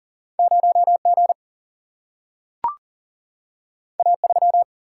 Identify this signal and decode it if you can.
Morse code, heard as a keyed tone, about 40 wpm, 700 Hz. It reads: 0 G A 3
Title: Single-Letter Single-Number 40wpm